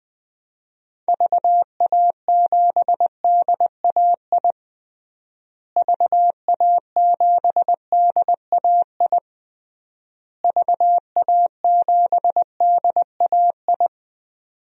Morse code, heard as a keyed tone, about 20 wpm, 700 Hz. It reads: VA7DAI VA7DAI VA7DAI
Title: DXCC-100 Call Signs - 6 Characters - Repeated 3x - 20wpm